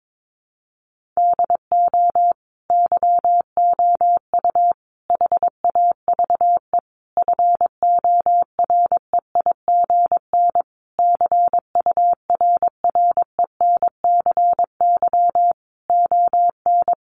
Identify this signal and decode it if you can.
Morse code, heard as a keyed tone, about 22 wpm, 700 Hz. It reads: DO YOU HAVE FOREIGN CURRENCY ON
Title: Sentences from Top 1500 Words 22wpm